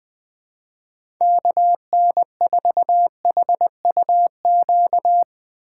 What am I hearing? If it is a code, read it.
Morse code, heard as a keyed tone, about 20 wpm, 700 Hz. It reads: KN4HUQ
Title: USA Call Signs - 6x Character Spacing - 20wpm